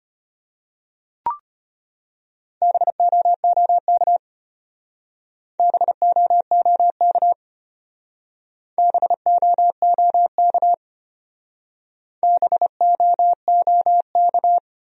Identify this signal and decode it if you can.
Morse code, heard as a keyed tone, about 32 wpm, 700 Hz. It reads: BOOK BOOK BOOK BOOK